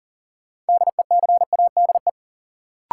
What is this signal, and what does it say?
Morse code, heard as a keyed tone, about 39 wpm, 700 Hz. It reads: DECADE
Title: Sets of 5 Words - New Word Drill - 40wpm